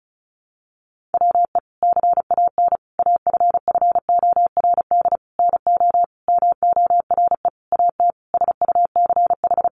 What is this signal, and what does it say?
Morse code, heard as a keyed tone, about 34 wpm, 700 Hz. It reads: WE CAN AFFORD NO MORE AT SUCH
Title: Sentences from Top 1100 Words 35wpm